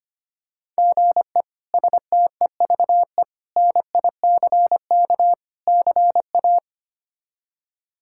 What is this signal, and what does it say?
Morse code, heard as a keyed tone, about 25 wpm, 700 Hz. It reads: GE STEVE NICK CA
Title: K1USN SST QSO Practice - 25wpm